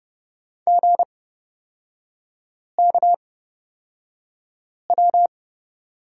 Morse code, heard as a keyed tone, about 30 wpm, spaced as for 5 wpm, 700 Hz. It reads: GKW
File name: Lesson 37e - Letter x - Three Characters 30wpm